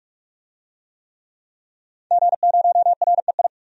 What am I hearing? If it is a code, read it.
Morse code, heard as a keyed tone, about 46 wpm, 700 Hz. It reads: G0REI
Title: DXCC-100 Call Signs - 5 Characters - 45wpm